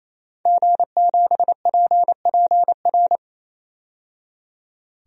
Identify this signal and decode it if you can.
Morse code, heard as a keyed tone, about 28 wpm, 700 Hz. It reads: G7PPR
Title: DXCC-100 Call Signs - 5 Characters - 28wpm